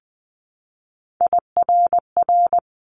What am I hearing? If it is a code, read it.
Morse code, heard as a keyed tone, about 20 wpm, 700 Hz. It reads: IRR
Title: Multi-letter Phonemes 20wpm